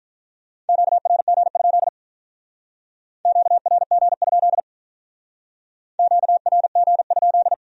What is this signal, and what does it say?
Morse code, heard as a keyed tone, about 44 wpm, 700 Hz. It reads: QRG? QRG? QRG?
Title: Speed-Racing Q-Codes - 35wpm